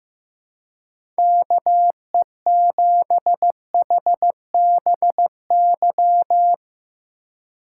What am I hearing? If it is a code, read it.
Morse code, heard as a keyed tone, about 15 wpm, 700 Hz. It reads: KE7HBY
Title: USA Call Signs - 6 Characters - 3x Character Spacing - 15wpm